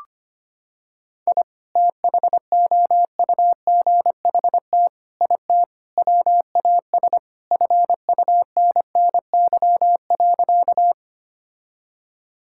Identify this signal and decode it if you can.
Morse code, heard as a keyed tone, about 25 wpm, 700 Hz. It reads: I THOUGHT IT WAS FUNNY.
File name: practice / 25wpm / Sentences with 4-Word Phrases - 25wpm